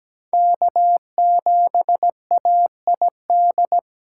Morse code, heard as a keyed tone, about 20 wpm, 700 Hz. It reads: K7AID
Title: Speed Racing - USA 1x3 Call Signs 17wpm